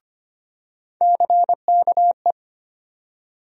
Morse code, heard as a keyed tone, about 25 wpm, 700 Hz. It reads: CKE
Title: USA Call Sign Postfixes - 3 Letters 25wpm